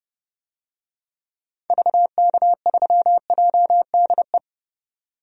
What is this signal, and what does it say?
Morse code, heard as a keyed tone, about 30 wpm, 700 Hz. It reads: VK3JDE
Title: DXCC-100 Call Signs - 6 Characters - 30wpm